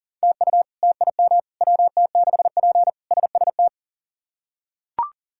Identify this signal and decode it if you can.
Morse code, heard as a keyed tone, about 40 wpm, 700 Hz. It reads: TU TIM WT6P SST
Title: K1USN SST QSO Practice - 40wpm